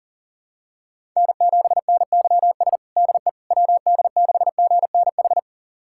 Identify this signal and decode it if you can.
Morse code, heard as a keyed tone, about 40 wpm, 700 Hz. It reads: N7NYS DE WD6GNH
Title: USA Callsign DE Callsign 40wpm